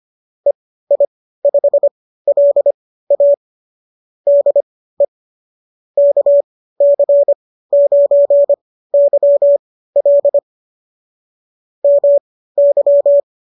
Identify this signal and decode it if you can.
Morse code, heard as a keyed tone, about 25 wpm, spaced as for 15 wpm, 590 Hz. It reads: EI5LA DE KC9YL MY